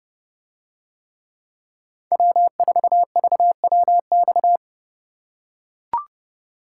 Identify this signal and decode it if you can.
Morse code, heard as a keyed tone, about 29 wpm, 700 Hz. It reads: W4VWX